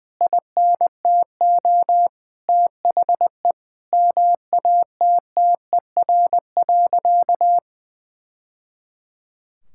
Morse code, heard as a keyed tone, about 20 wpm, 700 Hz. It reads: INTO THE MATTER.
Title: Sentences from Top 1000 Words - Review Part 6 - 20wpm